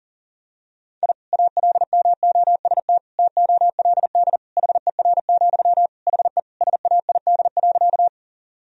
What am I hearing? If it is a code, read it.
Morse code, heard as a keyed tone, about 37 wpm, 700 Hz. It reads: I ALMOST TOLD HER, HE SAID.
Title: Sentences from Top 1000 Words - Review Part 4 - 40wpm